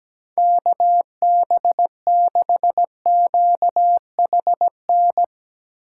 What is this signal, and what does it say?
Morse code, heard as a keyed tone, about 18 wpm, 700 Hz. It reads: KB6QHN
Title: Speed Racing - USA 2x3 Call Signs 17wpm